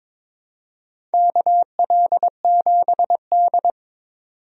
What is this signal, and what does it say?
Morse code, heard as a keyed tone, about 21 wpm, 700 Hz. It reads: KL7D
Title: DXCC-100 Call Signs - 4 Characters - 22wpm